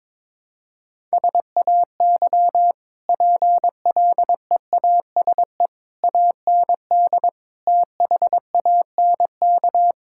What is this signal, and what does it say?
Morse code, heard as a keyed tone, about 22 wpm, 700 Hz. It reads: SAY PLEASE AND THANK
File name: practice / 22wpm / Easy Sentences with Hints - 22wpm